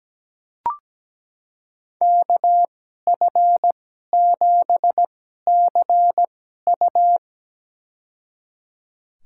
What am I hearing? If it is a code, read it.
Morse code, heard as a keyed tone, about 17 wpm, spaced as for 12 wpm, 700 Hz. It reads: KF7CU
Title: USA Call Signs - 2x Character Spacing - 17wpm